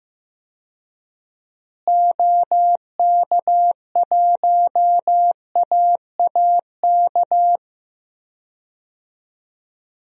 Morse code, heard as a keyed tone, about 15 wpm, 700 Hz. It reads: OK1AAK
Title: DXCC-100 Call Signs - 6 Characters - Repeated 2x - 15wpm